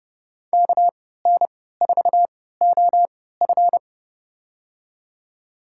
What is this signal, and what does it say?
Morse code, heard as a keyed tone, about 30 wpm, spaced as for 17 wpm, 700 Hz. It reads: KN4OF
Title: USA Call Signs - 3x Character Spacing - 30wpm